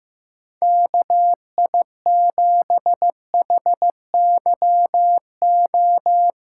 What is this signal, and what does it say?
Morse code, heard as a keyed tone, about 16 wpm, 700 Hz. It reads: KI7HYO
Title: USA Callsign DE Callsign - 6 Characters - 15wpm